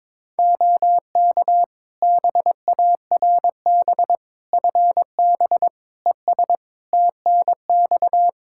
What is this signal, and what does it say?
Morse code, heard as a keyed tone, about 22 wpm, 700 Hz. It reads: OK BARB FB ES TNX
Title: Practice QSO - Exchange 2 - Info - 22wpm